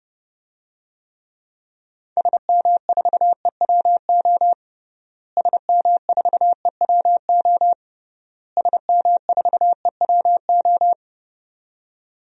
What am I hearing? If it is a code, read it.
Morse code, heard as a keyed tone, about 32 wpm, 700 Hz. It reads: SM4EWO SM4EWO SM4EWO
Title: DXCC-100 Call Signs - 6 Characters - Repeated 3x - 30wpm